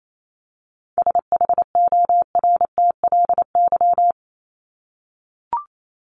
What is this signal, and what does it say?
Morse code, heard as a keyed tone, about 28 wpm, 700 Hz. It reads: SHORTLY
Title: LY Suffix - Top 100 Words with Base Words 28wpm